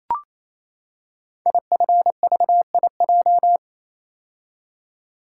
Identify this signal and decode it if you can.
Morse code, heard as a keyed tone, about 27 wpm, 700 Hz. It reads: IFVIJ
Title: Code Groups - 5 Letters - 28wpm